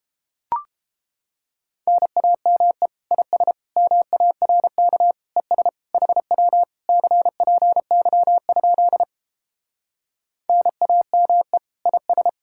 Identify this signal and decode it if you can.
Morse code, heard as a keyed tone, about 33 wpm, 700 Hz. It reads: NAME IS MARK ES HW CPY? NAME IS